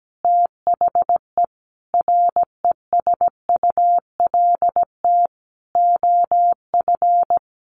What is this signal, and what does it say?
Morse code, heard as a keyed tone, about 17 wpm, 700 Hz. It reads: THE RESULT OF
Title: Sentences from Top 1500 Words 17wpm